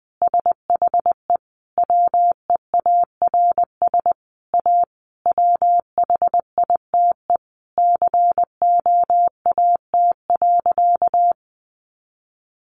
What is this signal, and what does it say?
Morse code, heard as a keyed tone, about 20 wpm, 700 Hz. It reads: SHE WEARS A WHITE COAT.
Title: Additional Easy Phrases 20wpm